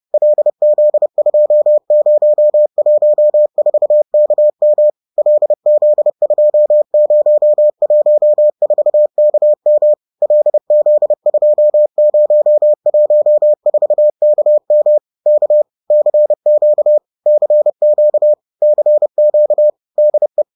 Morse code, heard as a keyed tone, about 30 wpm, 600 Hz. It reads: LZ2014KM LZ2014KM LZ2014KM K CQ CQ CQ DE
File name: 30wpm-7